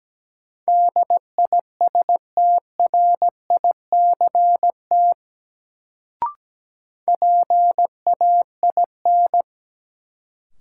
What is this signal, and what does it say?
Morse code, heard as a keyed tone, about 17 wpm, 700 Hz. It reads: DISTRICT PAIN
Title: Sentences from Top 1600 Words - New Word Drill Alt - 17wpm